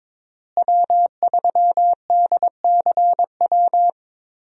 Morse code, heard as a keyed tone, about 22 wpm, 700 Hz. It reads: W3DCW
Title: USA Call Signs - 5 Characters - Repeated 3x - 22wpm